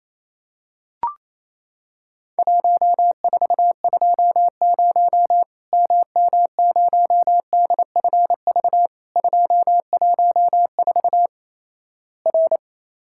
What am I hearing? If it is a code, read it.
Morse code, heard as a keyed tone, about 28 wpm, 700 Hz. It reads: 1420 MM0DFV 214 R
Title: Worked All Europe DX Contest - QTC Exchange - 28wpm